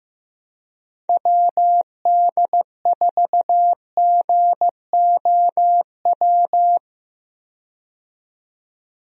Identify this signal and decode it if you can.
Morse code, heard as a keyed tone, about 15 wpm, 700 Hz. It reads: WD4GOW
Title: USA Call Signs - 6 Characters - 15wpm